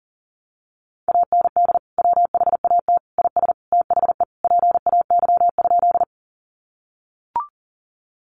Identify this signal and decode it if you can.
Morse code, heard as a keyed tone, about 40 wpm, 700 Hz. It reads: AND WHAT IS THE PAY?
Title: Sentences from Top 700 Words 40wpm